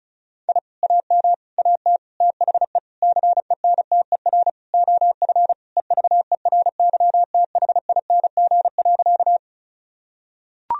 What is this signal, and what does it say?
Morse code, heard as a keyed tone, about 36 wpm, 700 Hz. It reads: I AM AT THE CENTER OF EVERYTHING.
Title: Sentences from Top 1000 Words - Review Part 2 - 35wpm